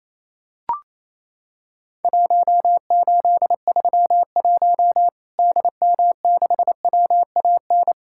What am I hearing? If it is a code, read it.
Morse code, heard as a keyed tone, about 28 wpm, 700 Hz. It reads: 1831 DM6WAN
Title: Worked All Europe DX Contest - QTC Exchange - 28wpm